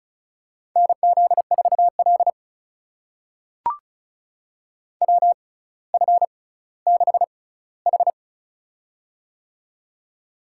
Morse code, heard as a keyed tone, about 36 wpm, spaced as for 18 wpm, 700 Hz. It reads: NZ4L WF6H
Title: USA Call Signs - 6x Character Spacing - 35wpm